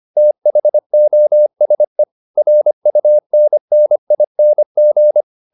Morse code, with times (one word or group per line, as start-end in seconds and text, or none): 0.2-2.0 THOSE
2.4-5.2 RUNNING